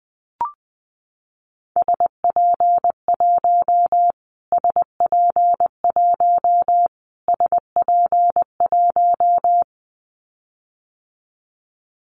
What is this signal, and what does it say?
Morse code, heard as a keyed tone, about 20 wpm, 700 Hz. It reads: SP1 SP1 SP1
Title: Callsign Prefixes for Top 25 DXCC - Repeated 3x - 20wpm